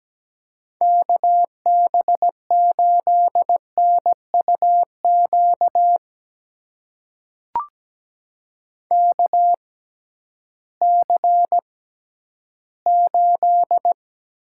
Morse code, heard as a keyed tone, about 17 wpm, spaced as for 11 wpm, 700 Hz. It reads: KB8NUQ KC8
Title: USA Call Signs - 6 Characters - 6x Character Spacing - 17wpm